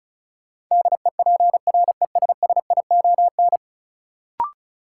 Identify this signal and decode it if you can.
Morse code, heard as a keyed tone, about 36 wpm, 700 Hz. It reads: DEPRESSION
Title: Sentences from Top 1300 Words - New Word Drill Alt - 35wpm